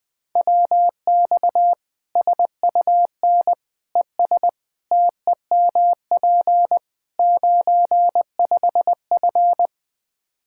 Morse code, heard as a keyed tone, about 20 wpm, 700 Hz. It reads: WX SUN ES TEMP 95F
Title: Practice QSO - Exchange 2 - Info - 20wpm